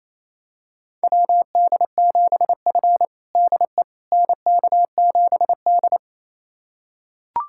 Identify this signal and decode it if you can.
Morse code, heard as a keyed tone, about 28 wpm, 700 Hz. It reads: WD7F DE NK7D
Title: USA Callsign DE Callsign - 4 Characters - 28wpm